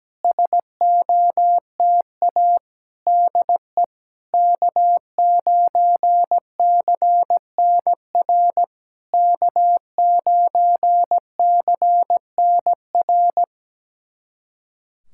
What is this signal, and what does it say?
Morse code, heard as a keyed tone, about 17 wpm, 690 Hz. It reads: SOTA DE K9CNR K9CNR
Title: SOTA QSO Practice 17wpm